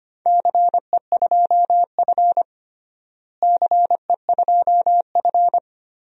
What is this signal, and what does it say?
Morse code, heard as a keyed tone, about 25 wpm, 700 Hz. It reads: CE2F CE2F
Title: DXCC-100 Call Signs - 4 Characters - Repeated 3x - 25wpm